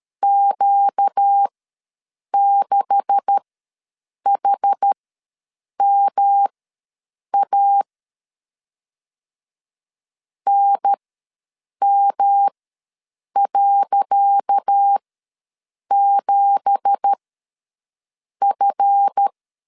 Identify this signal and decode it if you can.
Morse code, heard as a keyed tone, about 13 wpm, spaced as for 7 wpm, 800 Hz. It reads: Q6HMA NM.7F
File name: LEZIONI / 10 SIGNS / 10C DOT ws 12wpm 3548